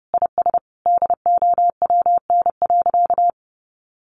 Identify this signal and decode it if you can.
Morse code, heard as a keyed tone, about 30 wpm, 700 Hz. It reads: IS DOWN.
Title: Sentences from Top 1100 Words 30wpm